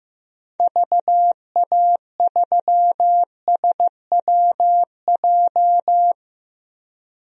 Morse code, heard as a keyed tone, about 15 wpm, 700 Hz. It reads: VA3SWJ
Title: DXCC-100 Call Signs - 6 Characters - 15wpm